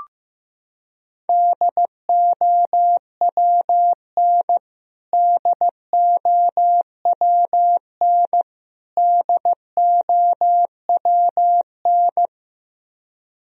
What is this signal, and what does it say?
Morse code, heard as a keyed tone, about 15 wpm, 700 Hz. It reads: DOWN DOWN DOWN
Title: Top 100 Words Repeated 15wpm